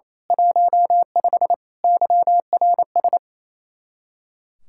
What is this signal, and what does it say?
Morse code, heard as a keyed tone, about 29 wpm, 700 Hz. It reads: 15 YRS
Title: Speed-Racing Common QSO Phrases - 28wpm